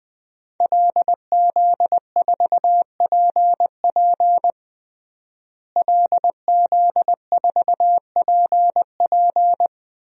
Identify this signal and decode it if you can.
Morse code, heard as a keyed tone, about 20 wpm, 700 Hz. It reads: LZ4PP LZ4PP
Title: DXCC-100 Call Signs - 5 Characters - Repeated 3x - 20wpm